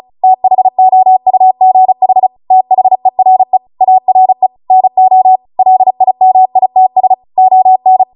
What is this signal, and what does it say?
Morse code, heard as a keyed tone, about 36 wpm, 750 Hz. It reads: THOUGH THERE ARE NO LIMITS ON
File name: Morse / 35WPM